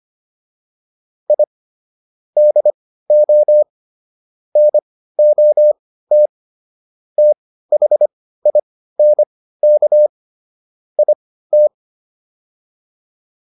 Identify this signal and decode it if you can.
Morse code, heard as a keyed tone, about 25 wpm, spaced as for 15 wpm, 600 Hz. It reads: I DO NOT THINK IT
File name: WD405 15